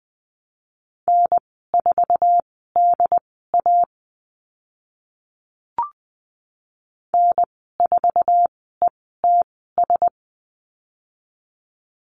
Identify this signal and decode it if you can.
Morse code, heard as a keyed tone, about 20 wpm, spaced as for 14 wpm, 700 Hz. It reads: N4DA N4ETS
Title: 069 Lesson 12g - Letter d - Call Signs 20wpm